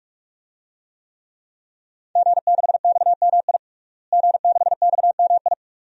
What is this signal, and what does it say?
Morse code, heard as a keyed tone, about 46 wpm, 700 Hz. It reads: G6XMI G6XMI
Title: DXCC-100 Call Signs - 5 Characters - Repeated 2x - 45wpm